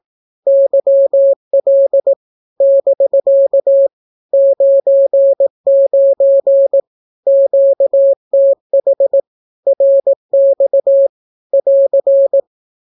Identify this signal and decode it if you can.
Morse code, heard as a keyed tone, about 18 wpm, 550 Hz. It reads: YL <BK> 99 QTH RX <AR>
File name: naukapodstaw / skroty / skroty